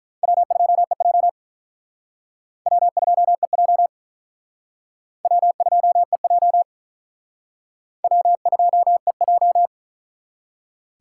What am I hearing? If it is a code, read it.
Morse code, heard as a keyed tone, about 44 wpm, 700 Hz. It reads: W2EJ W2EJ W2EJ W2EJ